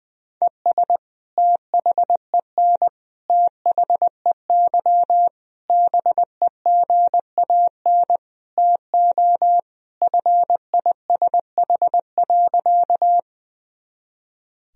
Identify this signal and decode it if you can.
Morse code, heard as a keyed tone, about 20 wpm, 700 Hz. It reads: ES THEN THEY BEGAN TO FISH.
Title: Sentences from Top 300 Words and Abbreviations 20wpm